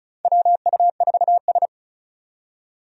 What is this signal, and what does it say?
Morse code, heard as a keyed tone, about 34 wpm, 700 Hz. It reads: WU4S
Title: USA Call Signs - 4 Characters - 35wpm